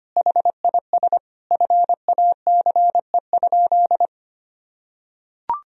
Morse code, heard as a keyed tone, about 25 wpm, 700 Hz. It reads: HIS FACE?
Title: Sentences from Top 1000 Words - Review Part 2 - 25wpm